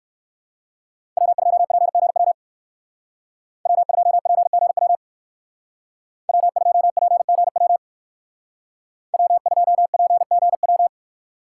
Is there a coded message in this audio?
Yes: Morse code from 1.2 to 10.9 s, heard as a keyed tone, about 58 wpm, 700 Hz. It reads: W2PGW W2PGW W2PGW W2PGW